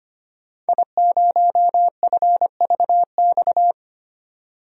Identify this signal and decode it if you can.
Morse code, heard as a keyed tone, about 25 wpm, 700 Hz. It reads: I0FVX
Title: DXCC-100 Call Signs - 5 Characters - 25wpm